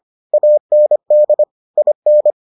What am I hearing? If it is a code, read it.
Morse code, heard as a keyed tone, about 25 wpm, 600 Hz. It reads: AND IN